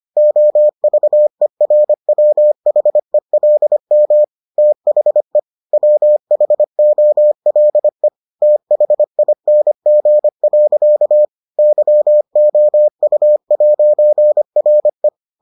Morse code, heard as a keyed tone, about 25 wpm, 600 Hz. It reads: OVERWHELM THE WHOLE THING. YOU'RE